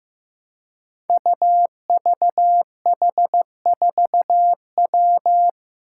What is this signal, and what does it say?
Morse code, heard as a keyed tone, about 15 wpm, 700 Hz. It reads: UVH4W
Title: Five Challenging Letter-Number Combinations 15wpm